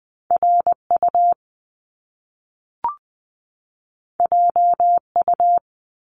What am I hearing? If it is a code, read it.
Morse code, heard as a keyed tone, about 20 wpm, 700 Hz. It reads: RU JU
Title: Top Two Letter Combinations in Words 20wpm